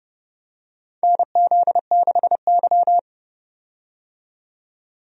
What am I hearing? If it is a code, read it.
Morse code, heard as a keyed tone, about 32 wpm, 700 Hz. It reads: NZ6Y
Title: USA Call Signs - 4 Characters - 30wpm